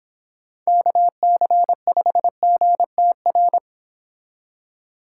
Morse code, heard as a keyed tone, about 25 wpm, 700 Hz. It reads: KC5GTR